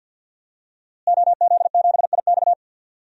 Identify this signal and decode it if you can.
Morse code, heard as a keyed tone, about 47 wpm, 700 Hz. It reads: OZ7IX
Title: DXCC-100 Call Signs - 5 Characters - Repeated 3x - 50wpm